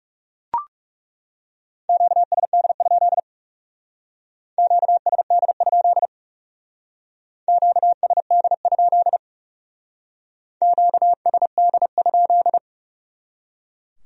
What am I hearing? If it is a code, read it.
Morse code, heard as a keyed tone, about 38 wpm, 700 Hz. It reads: QSD? QSD? QSD? QSD?